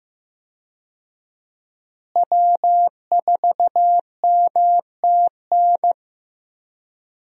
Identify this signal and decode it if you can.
Morse code, heard as a keyed tone, about 15 wpm, 700 Hz. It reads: W4MTN
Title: USA Call Signs - 5 Characters - Repeated 2x - 15wpm